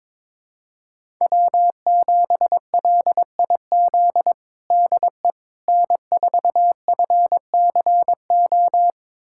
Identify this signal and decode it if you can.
Morse code, heard as a keyed tone, about 21 wpm, 700 Hz. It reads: W7LIZ DE N4FCO
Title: USA Callsign DE Callsign - 5 Characters - 22wpm